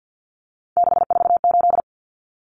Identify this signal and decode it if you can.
Morse code, heard as a keyed tone, about 50 wpm, 700 Hz. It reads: 648
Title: Numbers - 3-Digit 50wpm